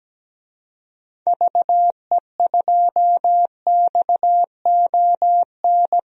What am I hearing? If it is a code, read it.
Morse code, heard as a keyed tone, about 17 wpm, 700 Hz. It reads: VE2XON
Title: DXCC-100 Call Signs - 6 Characters - Repeated 2x - 17wpm